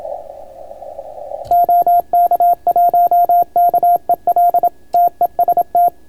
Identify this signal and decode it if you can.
Morse code, heard as a keyed tone, about 27 wpm, 690 Hz, with radio noise. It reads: OK1KEL TEST